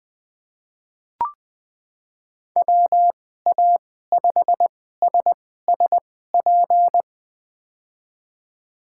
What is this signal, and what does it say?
Morse code, heard as a keyed tone, about 20 wpm, spaced as for 14 wpm, 700 Hz. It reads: WA5SSP
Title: USA Call Signs - 6 Characters - 2x Character Spacing - 20wpm